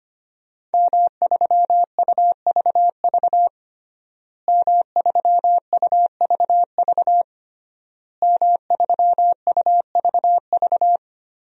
Morse code, heard as a keyed tone, about 25 wpm, 700 Hz. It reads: M3UVV M3UVV M3UVV
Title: DXCC-100 Call Signs - 5 Characters - Repeated 3x - 25wpm